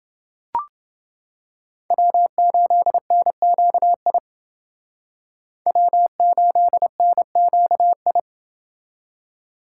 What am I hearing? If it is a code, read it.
Morse code, heard as a keyed tone, about 28 wpm, 700 Hz. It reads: W8NQI W8NQI